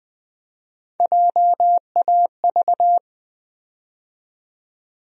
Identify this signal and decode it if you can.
Morse code, heard as a keyed tone, about 20 wpm, 700 Hz. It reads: JAV
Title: USA Call Sign Postfixes - 3 Letters 20wpm